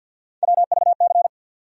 Morse code, heard as a keyed tone, about 46 wpm, 700 Hz. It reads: WVC
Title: USA Call Sign Postfixes - 3 Letters 50wpm